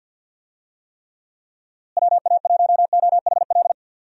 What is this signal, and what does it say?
Morse code, heard as a keyed tone, about 51 wpm, 700 Hz. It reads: WA1OHL
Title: USA Call Signs - 6 Characters - Repeated 2x - 50wpm